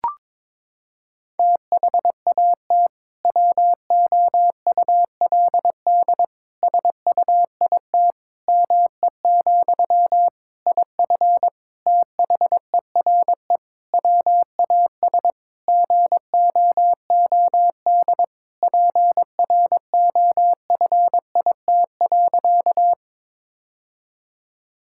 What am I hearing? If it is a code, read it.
Morse code, heard as a keyed tone, about 22 wpm, 700 Hz. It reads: THAT WOULD SUIT ME, IF THERE WAS GOOD PROFIT.
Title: Sentences from Top 1600 Words 22wpm